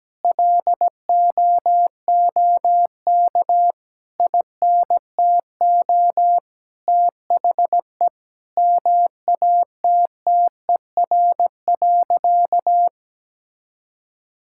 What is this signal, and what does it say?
Morse code, heard as a keyed tone, about 17 wpm, 700 Hz. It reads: LOOK INTO THE MATTER.
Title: Sentences from Top 1000 Words - Review Part 6 - 17wpm